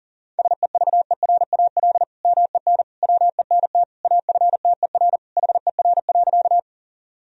Morse code, heard as a keyed tone, about 40 wpm, 700 Hz. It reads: SEVERAL MEN WENT AFTER HER.
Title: Sentences from Top 1000 Words - Review Part 2 - 40wpm